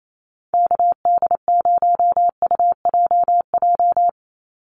Morse code, heard as a keyed tone, about 28 wpm, 700 Hz. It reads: KD0UJJ
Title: USA Call Signs 28wpm